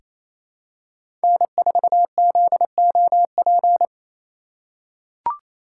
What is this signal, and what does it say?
Morse code, heard as a keyed tone, about 28 wpm, 700 Hz. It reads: N4ZOP
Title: USA Call Signs - 5 Characters - 2x Character Spacing - 28wpm